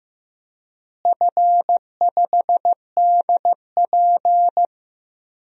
Morse code, heard as a keyed tone, about 15 wpm, 700 Hz. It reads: F5DP
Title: DXCC-100 Call Signs - 4 Characters - Repeated 3x - 15wpm